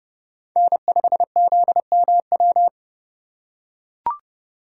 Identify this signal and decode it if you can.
Morse code, heard as a keyed tone, about 27 wpm, 700 Hz. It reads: N5ZMW